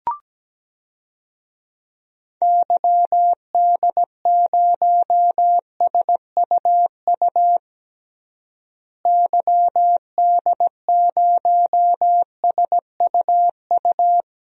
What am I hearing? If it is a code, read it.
Morse code, heard as a keyed tone, about 17 wpm, 700 Hz. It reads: YD0SUU YD0SUU
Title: DXCC-100 Call Signs - 6 Characters - Repeated 2x - 17wpm